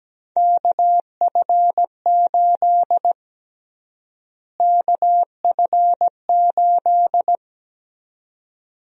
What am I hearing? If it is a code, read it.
Morse code, heard as a keyed tone, about 17 wpm, 700 Hz. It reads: KF8 KF8
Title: USA Call Sign Prefixes - Repeated 2x - 17wpm